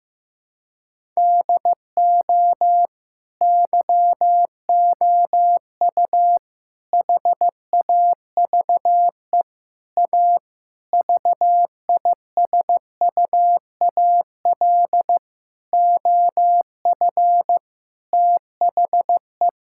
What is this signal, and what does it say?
Morse code, heard as a keyed tone, about 15 wpm, 700 Hz. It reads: DO YOU HAVE A VISUAL OF THE
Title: Sentences from Top 1000 Words - Review Part 6 - 15wpm